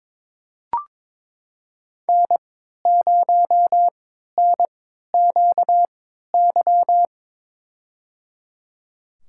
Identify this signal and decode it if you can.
Morse code, heard as a keyed tone, about 22 wpm, spaced as for 13 wpm, 700 Hz. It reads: N0NQY
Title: USA Call Signs - 5 Characters - 3x Character Spacing - 22wpm